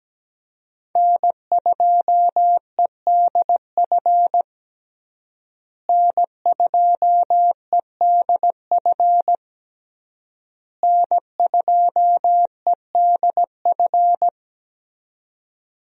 Morse code, heard as a keyed tone, about 17 wpm, 700 Hz. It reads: N2EDF N2EDF N2EDF
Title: USA Call Signs - 5 Characters - Repeated 3x - 17wpm